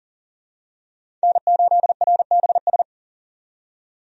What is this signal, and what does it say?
Morse code, heard as a keyed tone, about 40 wpm, 700 Hz. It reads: N8RBS
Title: USA Call Signs - 5 Characters - 40wpm